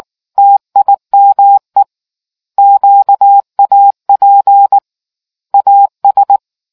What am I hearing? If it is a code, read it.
Morse code, heard as a keyed tone, about 19 wpm, 800 Hz. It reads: TIME QAP AS